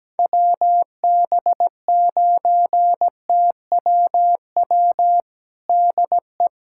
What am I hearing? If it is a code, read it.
Morse code, heard as a keyed tone, about 17 wpm, 700 Hz. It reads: WB9TWW DE
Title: USA Callsign DE Callsign - 6 Characters - 17wpm